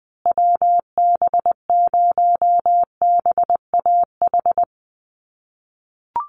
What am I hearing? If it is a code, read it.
Morse code, heard as a keyed tone, about 20 wpm, 700 Hz. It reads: WB0BAH
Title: USA Call Signs Repeated 3x 20wpm